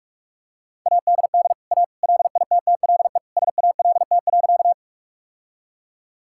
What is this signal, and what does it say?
Morse code, heard as a keyed tone, about 46 wpm, 700 Hz. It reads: ADD A LITTLE SALT.
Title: Sentences from Top 1000 Words - Review Part 5 - 45wpm